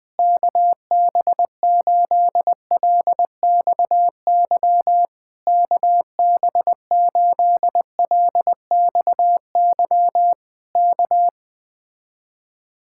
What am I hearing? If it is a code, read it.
Morse code, heard as a keyed tone, about 20 wpm, 700 Hz. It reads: KB8LXY KB8LXY K